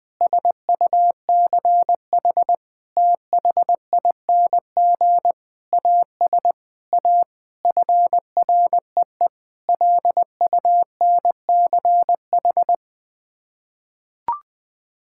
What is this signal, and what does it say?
Morse code, heard as a keyed tone, about 20 wpm, 700 Hz. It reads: SUCH THING AS A FREE LUNCH
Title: Sets of 9 Words - 20wpm